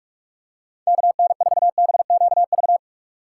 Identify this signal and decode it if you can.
Morse code, heard as a keyed tone, about 45 wpm, 700 Hz. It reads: KN4BQV
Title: USA Call Signs - 6 Characters - 3x Character Spacing - 45wpm